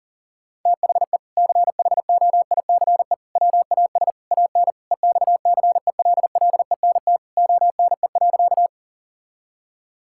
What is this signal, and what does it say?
Morse code, heard as a keyed tone, about 37 wpm, 700 Hz. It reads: THE CHOICE WAS AN EXCELLENT ONE.
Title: Sentences from Top 1000 Words - Review Part 6 - 40wpm